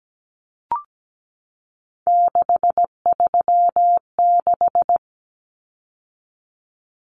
Morse code, heard as a keyed tone, about 17 wpm, 700 Hz. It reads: 636
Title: Numbers - 3-Digit 17wpm